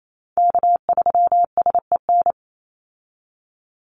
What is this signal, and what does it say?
Morse code, heard as a keyed tone, about 28 wpm, 700 Hz. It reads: K3SEN
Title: Call Signs Repeated 3x - 28wpm